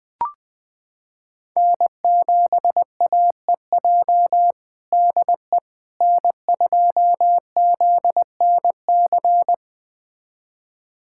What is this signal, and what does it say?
Morse code, heard as a keyed tone, about 20 wpm, 700 Hz. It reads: N7AEJ DE N2ZNC
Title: USA Callsign DE Callsign 20wpm